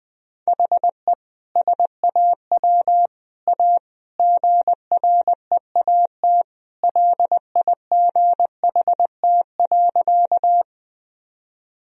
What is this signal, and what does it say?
Morse code, heard as a keyed tone, about 20 wpm, 700 Hz. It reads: HE SAW A GREAT LIGHT.
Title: Sentences from Top 1000 Words - Review Part 1 - 20wpm